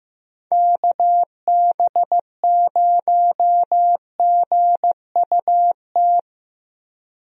Morse code, heard as a keyed tone, about 15 wpm, 700 Hz. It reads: KB0GUT
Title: USA Call Signs - 6 Characters - Repeated 3x - 15wpm